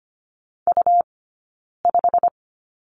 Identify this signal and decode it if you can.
Morse code, heard as a keyed tone, about 25 wpm, 700 Hz. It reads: U 5
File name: Lesson 17d - Letter u - Two Characters 25wpm